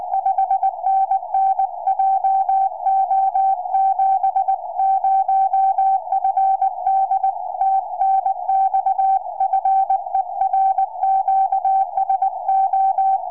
Morse code, heard as a keyed tone, about 19 wpm, 770 Hz. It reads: 5NN JO70FD TNX FER QSO
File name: tropo / ua2fl 2m